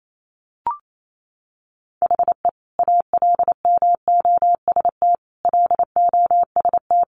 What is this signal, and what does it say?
Morse code, heard as a keyed tone, about 28 wpm, 700 Hz. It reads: HE ALMOST LOST
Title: Sentences from Top 1400 Words 28wpm